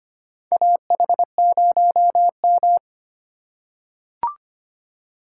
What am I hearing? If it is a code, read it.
Morse code, heard as a keyed tone, about 25 wpm, 700 Hz. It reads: AH0M
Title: USA Call Signs - 4 Characters - 3x Character Spacing - 25wpm